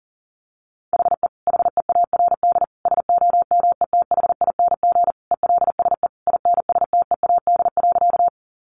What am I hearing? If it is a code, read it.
Morse code, heard as a keyed tone, about 37 wpm, 700 Hz. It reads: HE HEARD SOMETHING ELSE INSTEAD.
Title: Sentences from Top 800 Words 40wpm